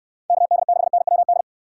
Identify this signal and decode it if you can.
Morse code, heard as a keyed tone, about 69 wpm, 700 Hz. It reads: KD6NWD